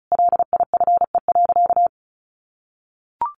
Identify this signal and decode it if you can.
Morse code, heard as a keyed tone, about 34 wpm, 700 Hz. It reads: LIFE.
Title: Sentences from Top 1400 Words 35wpm